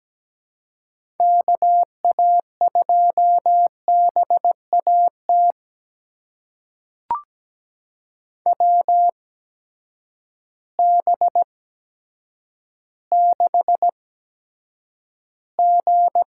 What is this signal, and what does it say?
Morse code, heard as a keyed tone, about 17 wpm, spaced as for 8 wpm, 700 Hz. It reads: KA2BAT WB6G
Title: USA Call Signs - 6 Characters - 8x Character Spacing - 17wpm